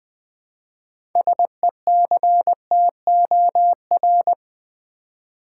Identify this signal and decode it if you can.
Morse code, heard as a keyed tone, about 20 wpm, 700 Hz. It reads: SECTOR
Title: Build A Word - 2 Letters At A Time - 6-Letter Words - 20wpm